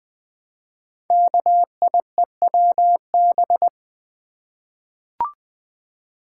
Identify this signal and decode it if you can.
Morse code, heard as a keyed tone, about 20 wpm, 700 Hz. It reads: KIEWB
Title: Code Groups - 5 Letters - 20wpm